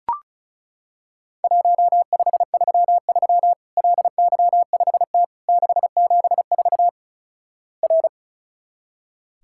Morse code, heard as a keyed tone, about 36 wpm, 700 Hz. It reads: 1533 LY5T 674 R
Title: Worked All Europe DX Contest - QTC Exchange - 35wpm